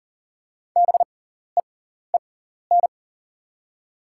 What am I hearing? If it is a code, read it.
Morse code, heard as a keyed tone, about 40 wpm, spaced as for 14 wpm, 690 Hz. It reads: BEEN
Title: 6x Character Spacing - Top 100 Words - 40wpm